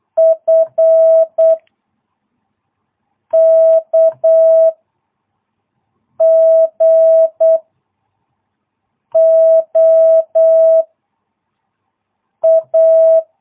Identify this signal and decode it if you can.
Morse code, heard as a keyed tone, about 8 wpm, 660 Hz. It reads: F K G O A